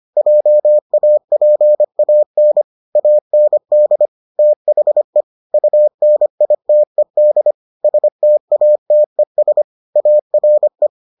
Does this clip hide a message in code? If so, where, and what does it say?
Yes: Morse code from 0.2 to 10.9 s, heard as a keyed tone, about 25 wpm, 600 Hz. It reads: JAPAN AND THE UNITED STATES ARE